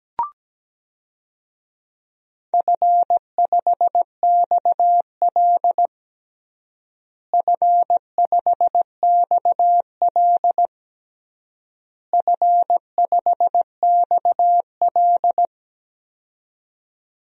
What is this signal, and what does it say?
Morse code, heard as a keyed tone, about 17 wpm, 700 Hz. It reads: F5XL F5XL F5XL
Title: DXCC-100 Call Signs - 4 Characters - Repeated 3x - 17wpm